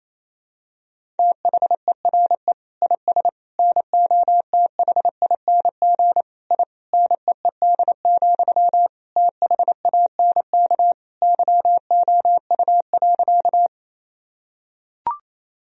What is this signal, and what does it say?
Morse code, heard as a keyed tone, about 28 wpm, 700 Hz. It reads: THERE IS NOTHING I NEED, THANK YOU.
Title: Sentences from Top 1000 Words - Review Part 1 - 28wpm